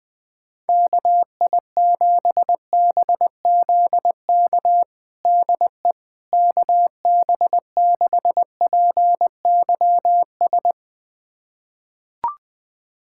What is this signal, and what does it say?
Morse code, heard as a keyed tone, about 21 wpm, 700 Hz. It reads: KI7BZK DE KB6PYS
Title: USA Callsign DE Callsign - 6 Characters - 20wpm